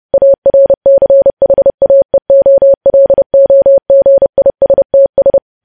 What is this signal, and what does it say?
Morse code, heard as a keyed tone, about 30 wpm, 550 Hz. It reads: ARCHAEOLOGISTS
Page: news headlines in morse code at 30 wpm